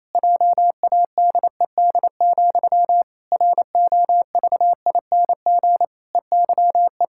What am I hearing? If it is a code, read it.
Morse code, heard as a keyed tone, about 27 wpm, 700 Hz. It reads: JADED, ROVING EYE
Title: Pangram Challenge v2 - 28wpm